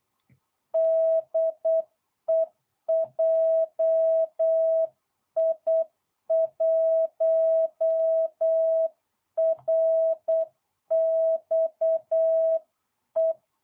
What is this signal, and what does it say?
Morse code, heard as a keyed tone, about 8 wpm, 650 Hz. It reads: DE JI1RXE